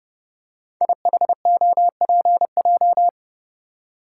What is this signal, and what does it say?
Morse code, heard as a keyed tone, about 28 wpm, 700 Hz. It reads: IHOPJ
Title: Code Groups - 5 Letters - 30wpm